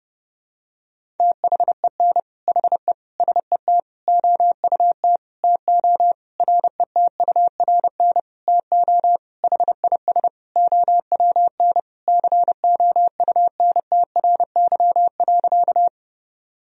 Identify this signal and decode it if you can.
Morse code, heard as a keyed tone, about 30 wpm, 700 Hz. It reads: THEN HE SET OUT TO RETURN TO HIS OWN COUNTRY.
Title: Sentences from Top 1000 Words - Review Part 1 - 30wpm